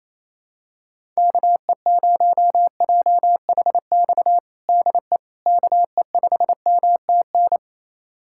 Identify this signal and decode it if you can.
Morse code, heard as a keyed tone, about 28 wpm, 700 Hz. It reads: KE0JHX DE KE5MTN
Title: USA Callsign DE Callsign - 6 Characters - 28wpm